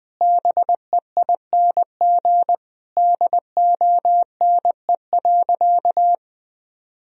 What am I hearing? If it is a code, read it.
Morse code, heard as a keyed tone, about 20 wpm, 700 Hz. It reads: BEING DONE.
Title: Sentences from Top 1000 Words - Review Part 3 - 20wpm